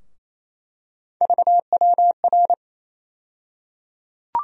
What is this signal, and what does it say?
Morse code, heard as a keyed tone, about 28 wpm, 700 Hz. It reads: VWR
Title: USA Call Sign Postfixes - 3 Letters 28wpm